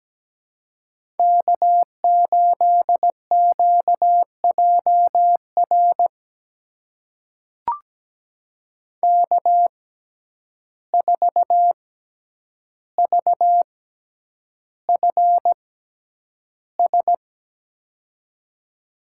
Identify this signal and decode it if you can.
Morse code, heard as a keyed tone, about 17 wpm, spaced as for 9 wpm, 700 Hz. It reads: K8QJR K4VFS
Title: USA Call Signs - 5 Characters - 6x Character Spacing - 17wpm